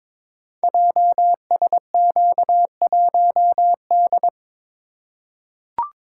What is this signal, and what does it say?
Morse code, heard as a keyed tone, about 23 wpm, 700 Hz. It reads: JSQ1D
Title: Code Groups - 5 Letter:Numbers - 22wpm